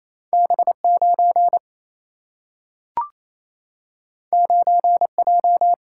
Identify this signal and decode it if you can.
Morse code, heard as a keyed tone, about 28 wpm, 700 Hz. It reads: B 9 9 J
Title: Single-Letter Single-Number 28wpm